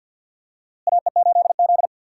Morse code, heard as a keyed tone, about 48 wpm, 700 Hz. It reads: AE8Z
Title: USA Call Signs - 4 Characters - Repeated 3x - 50wpm